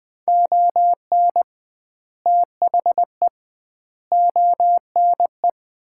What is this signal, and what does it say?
Morse code, heard as a keyed tone, about 20 wpm, 700 Hz. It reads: ON THE ONE
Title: 2x Word-Spacing - Sets of 4 Words - 20wpm